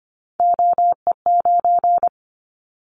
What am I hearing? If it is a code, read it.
Morse code, heard as a keyed tone, about 25 wpm, 700 Hz. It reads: OE9
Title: Callsign Prefixes for Top 100 DXCC - 25wpm